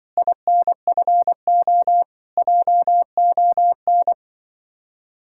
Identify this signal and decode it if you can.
Morse code, heard as a keyed tone, about 24 wpm, 700 Hz. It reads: INFO JON
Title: Speed-Racing Common QSO Phrases Encore - 20wpm